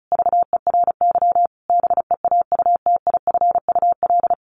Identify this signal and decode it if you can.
Morse code, heard as a keyed tone, about 34 wpm, 700 Hz. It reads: VERY BEAUTIFUL
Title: A Sound Of Thunder 35wpm